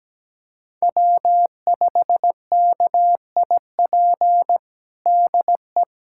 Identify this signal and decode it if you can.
Morse code, heard as a keyed tone, about 17 wpm, 700 Hz. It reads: W5KIP DE
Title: USA Callsign DE Callsign - 5 Characters - 17wpm